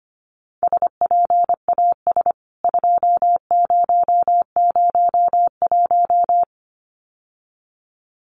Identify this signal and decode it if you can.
Morse code, heard as a keyed tone, about 25 wpm, 700 Hz. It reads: SPAS 2001
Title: CWT Exchange 25wpm